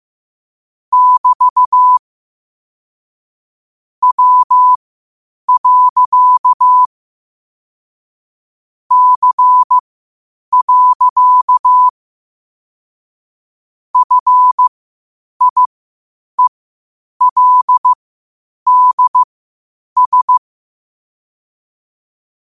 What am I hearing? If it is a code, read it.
Morse code, heard as a keyed tone, about 15 wpm, spaced as for 8 wpm, 1000 Hz. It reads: <BT> W. C. FIELDS
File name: qotdfiles / 8wpm